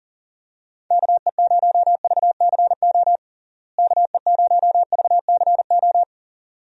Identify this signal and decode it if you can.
Morse code, heard as a keyed tone, about 40 wpm, 700 Hz. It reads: KE0VCO KE0VCO
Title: USA Call Signs - 6 Characters - Repeated 2x - 40wpm